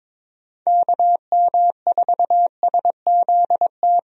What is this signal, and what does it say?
Morse code, heard as a keyed tone, about 22 wpm, 700 Hz. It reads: KM4SZT
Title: USA Call Signs - 6 Characters - Repeated 3x - 22wpm